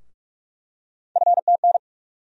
Morse code, heard as a keyed tone, about 46 wpm, 700 Hz. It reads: FTN
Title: USA Call Sign Postfixes - 3 Letters 45wpm